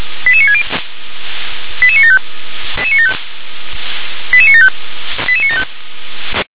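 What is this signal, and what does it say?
Five-tone selective call (ZVEI), heard as quick runs of five tones, tones of about 70 ms. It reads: 88960 88975 88975 88975
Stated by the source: Received by SDR.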